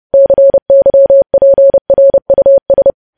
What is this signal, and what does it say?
Morse code, heard as a keyed tone, about 30 wpm, 550 Hz. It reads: CYPRUS
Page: news headlines in morse code at 30 wpm